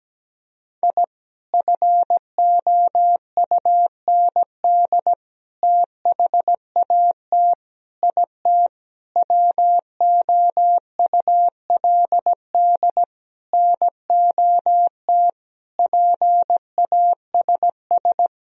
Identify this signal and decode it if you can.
Morse code, heard as a keyed tone, about 17 wpm, 700 Hz. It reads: I FOUND THAT IT WOULD NOT PASS
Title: Sentences from Top 1000 Words - Review Part 1 - 17wpm